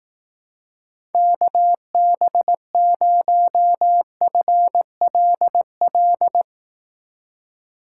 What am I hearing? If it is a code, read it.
Morse code, heard as a keyed tone, about 19 wpm, 700 Hz. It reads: KB0FLL
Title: Speed Racing - USA 2x3 Call Signs 15wpm